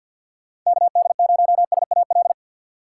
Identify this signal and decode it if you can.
Morse code, heard as a keyed tone, about 50 wpm, 700 Hz. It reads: KD0SAL
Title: USA Call Signs - 6 Characters - 6x Character Spacing - 50wpm